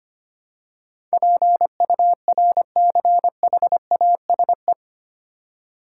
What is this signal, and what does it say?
Morse code, heard as a keyed tone, about 25 wpm, 690 Hz. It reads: PURCHASE
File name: 8 Letter Words - First 3 Letters Given - 25wpm